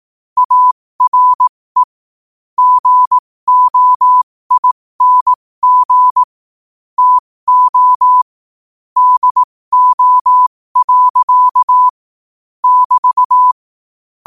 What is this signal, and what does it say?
Morse code, heard as a keyed tone, about 18 wpm, spaced as for 15 wpm, 1000 Hz. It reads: ARE GOING TO DO. <BT>
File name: qotdfiles / 15wpm